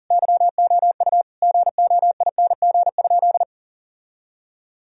Morse code, heard as a keyed tone, about 40 wpm, 700 Hz. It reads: YOU GOING?
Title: Sentences from Top 1000 Words - Review Part 2 - 40wpm